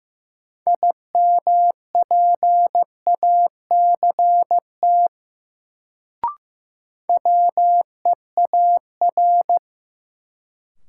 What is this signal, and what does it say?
Morse code, heard as a keyed tone, about 15 wpm, 700 Hz. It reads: IMPACT WEAR
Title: Sentences from Top 1000 Words - Review Part 6 - New Word Drill - 15wpm